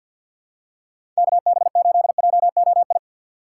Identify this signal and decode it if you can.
Morse code, heard as a keyed tone, about 53 wpm, 690 Hz. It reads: KB8JOI
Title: USA Call Signs - 6x Character Spacing - 50wpm